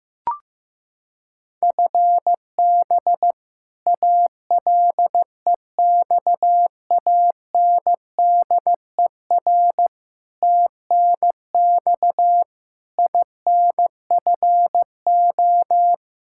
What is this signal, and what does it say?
Morse code, heard as a keyed tone, about 15 wpm, 700 Hz. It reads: FB ALEXANDER TNX INFO
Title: K1USN SST QSO Practice 15wpm